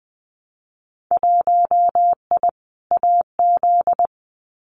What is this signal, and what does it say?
Morse code, heard as a keyed tone, about 20 wpm, 700 Hz. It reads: 1I AZ
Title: Winter Field Day Exchanges 20wpm